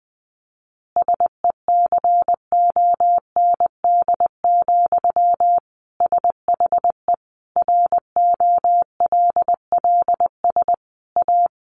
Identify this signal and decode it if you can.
Morse code, heard as a keyed tone, about 19 wpm, 700 Hz. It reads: SECOND, SHE ROLLS A
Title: Easy Phrases 20wpm